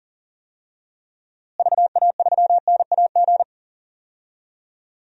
Morse code, heard as a keyed tone, about 40 wpm, 700 Hz. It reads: VA3NAG
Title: DXCC-100 Call Signs - 6 Characters - Repeated 3x - 40wpm